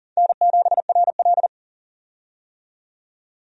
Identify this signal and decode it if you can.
Morse code, heard as a keyed tone, about 40 wpm, 700 Hz. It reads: N7RL
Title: USA Call Signs - 4 Characters - 40wpm